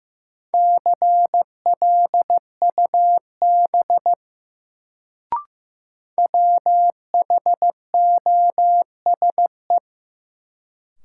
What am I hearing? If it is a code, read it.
Morse code, heard as a keyed tone, about 15 wpm, 700 Hz. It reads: CLUB WHOSE
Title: Sentences from Top 1200 Words - New Word Drill Alt - 15wpm